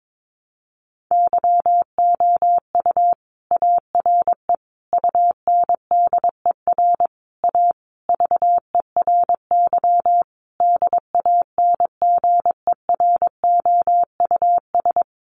Morse code, heard as a keyed tone, about 22 wpm, 700 Hz. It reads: YOU ARE UNDER A VERY DANGEROUS
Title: Sentences from Top 1200 Words 22wpm